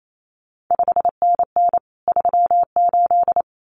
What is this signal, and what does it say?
Morse code, heard as a keyed tone, about 28 wpm, 700 Hz. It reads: 5NN 38
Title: CQ WW DX Contest Exchange Encore - 28wpm